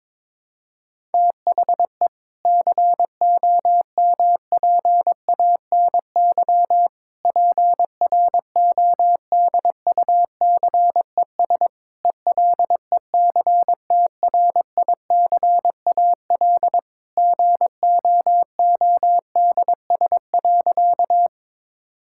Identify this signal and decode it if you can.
Morse code, heard as a keyed tone, about 22 wpm, 700 Hz. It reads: THE COMPANY PRODUCES ELECTRICAL GOODS.
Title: Sentences from Top 1000 Words - Review Part 6 - 22wpm